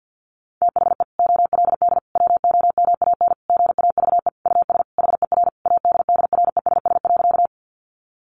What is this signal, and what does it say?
Morse code, heard as a keyed tone, about 50 wpm, 700 Hz. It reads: THE OLD WOMAN GAVE US HER ADDRESS.
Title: Sentences from Top 1100 Words 50wpm